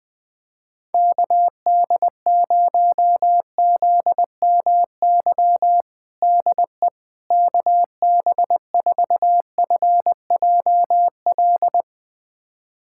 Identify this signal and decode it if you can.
Morse code, heard as a keyed tone, about 20 wpm, 700 Hz. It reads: KD0ZMY DE KB4FJL
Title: USA Callsign DE Callsign - 6 Characters - 20wpm